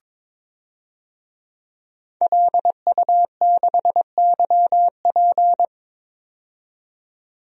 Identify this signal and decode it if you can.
Morse code, heard as a keyed tone, about 22 wpm, 700 Hz. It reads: LU6YP
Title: DXCC-100 Call Signs - 5 Characters - 22wpm